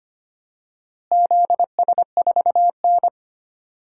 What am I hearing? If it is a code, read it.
Morse code, heard as a keyed tone, about 25 wpm, 700 Hz. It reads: ZS4N
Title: DXCC-100 Call Signs - 4 Characters - Repeated 3x - 25wpm